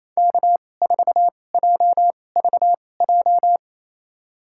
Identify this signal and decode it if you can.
Morse code, heard as a keyed tone, about 28 wpm, spaced as for 20 wpm, 700 Hz. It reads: K4JVJ
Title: USA Call Signs - 5 Characters - 2x Character Spacing - 28wpm